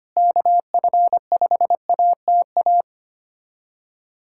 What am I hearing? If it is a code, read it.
Morse code, heard as a keyed tone, about 25 wpm, 700 Hz. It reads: KF5ATA
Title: USA Callsign DE Callsign - 6 Characters - 25wpm